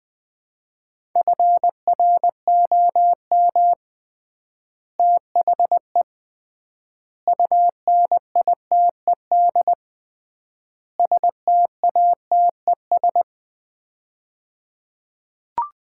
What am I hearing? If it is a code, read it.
Morse code, heard as a keyed tone, about 21 wpm, 700 Hz. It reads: FROM THE UNITED STATES
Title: 3x Word-Spacing - Sets of 4 Words - 20wpm